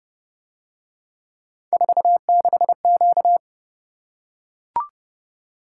Morse code, heard as a keyed tone, about 27 wpm, 700 Hz. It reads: 46Q